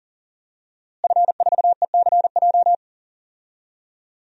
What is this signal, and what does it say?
Morse code, heard as a keyed tone, about 39 wpm, 700 Hz. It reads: F4ECJ
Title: DXCC-100 Call Signs - 5 Characters - Repeated 2x - 40wpm